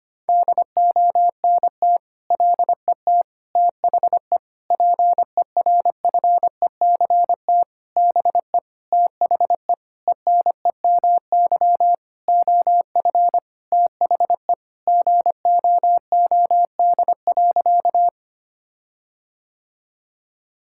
Morse code, heard as a keyed tone, about 25 wpm, 700 Hz. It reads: DONT LET THE PERFECT BE THE ENEMY OF THE GOOD.
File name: practice / 25wpm / Sentences with 9-Word Phrases - 25wpm